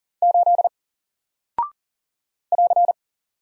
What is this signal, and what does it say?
Morse code, heard as a keyed tone, about 40 wpm, 700 Hz. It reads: 8 <AR>
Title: Single Character (Letter:Number:Punctuation:Prosign) 40wpm